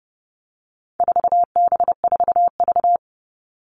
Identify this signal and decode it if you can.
Morse code, heard as a keyed tone, about 30 wpm, 700 Hz. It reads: 4B4V
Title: All Cut Numbers - 4-Digit 30wpm